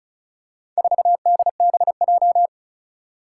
Morse code, heard as a keyed tone, about 36 wpm, 700 Hz. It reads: 4DBJ
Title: Four Challenging Letter-Number Combinations 35wpm